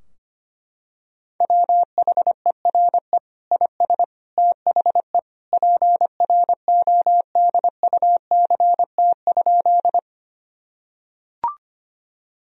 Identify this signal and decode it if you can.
Morse code, heard as a keyed tone, about 25 wpm, 700 Hz. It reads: WHERE IS THE PRODUCT?
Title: Sentences from Top 1000 Words - Review Part 2 - 25wpm